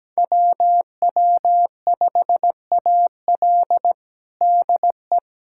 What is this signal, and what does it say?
Morse code, heard as a keyed tone, about 17 wpm, 700 Hz. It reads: WW5AL DE
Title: USA Callsign DE Callsign - 5 Characters - 17wpm